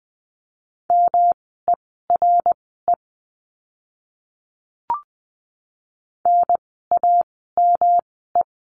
Morse code, heard as a keyed tone, about 20 wpm, spaced as for 14 wpm, 700 Hz. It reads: MERE NAME
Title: 108 Lesson 18f - Letter m - Words 20wpm